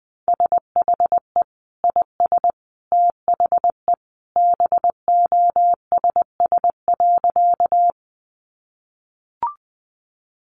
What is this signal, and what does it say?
Morse code, heard as a keyed tone, about 20 wpm, 700 Hz. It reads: SHE IS THE BOSS.
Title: Sentences from Top 700 Words 20wpm